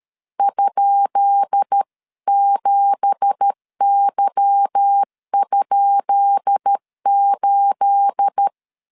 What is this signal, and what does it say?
Morse code, heard as a keyed tone, about 13 wpm, 800 Hz. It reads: ?7Y?8
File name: LEZIONI / 10 SIGNS / 10D COMA 12wpm 2544